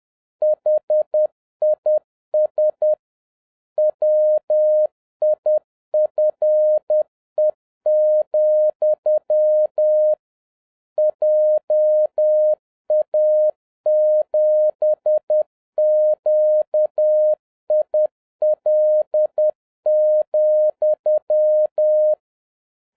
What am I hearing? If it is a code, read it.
Morse code, heard as a keyed tone, about 10 wpm, 610 Hz. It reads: HIS WIFE, JA7QIL,
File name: 2011-07-mp3 / n46 JOHN K3TN 10wpm